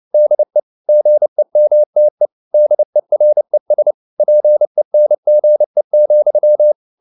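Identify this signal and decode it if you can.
Morse code, heard as a keyed tone, about 29 wpm, 600 Hz. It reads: DE GEMTE DERES PENGE,
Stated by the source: Fast (QRQ) Morse at 29 wpm, from CW med Gnister.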